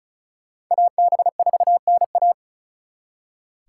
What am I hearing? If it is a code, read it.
Morse code, heard as a keyed tone, about 46 wpm, 700 Hz. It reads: AB4NA